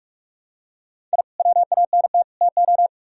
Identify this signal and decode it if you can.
Morse code, heard as a keyed tone, about 45 wpm, 700 Hz. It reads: I WANT TO